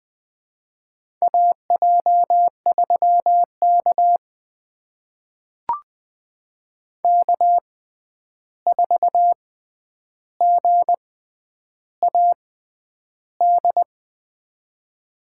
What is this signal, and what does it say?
Morse code, heard as a keyed tone, about 20 wpm, spaced as for 10 wpm, 700 Hz. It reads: AJ3K K4GAD
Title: USA Call Signs - 6x Character Spacing - 20wpm